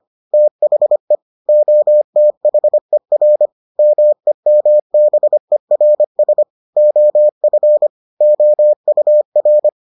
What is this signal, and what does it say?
Morse code, heard as a keyed tone, about 25 wpm, 600 Hz. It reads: THE OTHER MEMBERS OF OUR